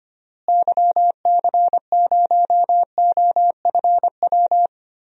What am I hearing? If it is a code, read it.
Morse code, heard as a keyed tone, about 25 wpm, 700 Hz. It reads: YC0OFW
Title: DXCC-100 Call Signs - 6 Characters - Repeated 3x - 25wpm